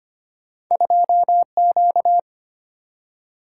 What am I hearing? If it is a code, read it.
Morse code, heard as a keyed tone, about 25 wpm, 700 Hz. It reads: 2Q
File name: Two Letter-Number Combinations - 25wpm